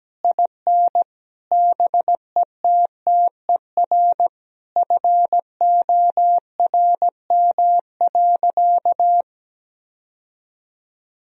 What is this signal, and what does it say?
Morse code, heard as a keyed tone, about 17 wpm, 700 Hz. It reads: IN BETTER FORM.
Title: Sentences from Top 1000 Words - Review Part 1 - 17wpm